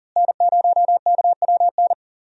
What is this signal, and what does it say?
Morse code, heard as a keyed tone, about 40 wpm, 700 Hz. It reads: N0KWN
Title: USA Call Signs - 5 Characters - Repeated 2x - 40wpm